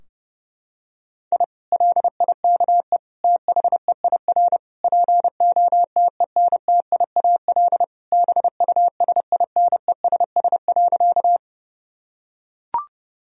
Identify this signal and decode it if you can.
Morse code, heard as a keyed tone, about 32 wpm, 700 Hz. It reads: I LIKE THEIR POTENTIAL BUSINESS.
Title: Sentences from Top 1000 Words - Review Part 4 - 30wpm